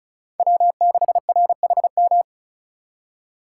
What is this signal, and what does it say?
Morse code, heard as a keyed tone, about 45 wpm, 700 Hz. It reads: W6RHM